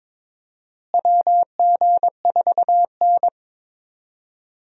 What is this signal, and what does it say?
Morse code, heard as a keyed tone, about 22 wpm, 700 Hz. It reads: WG4N
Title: USA Call Signs - 4 Characters - Repeated 3x - 22wpm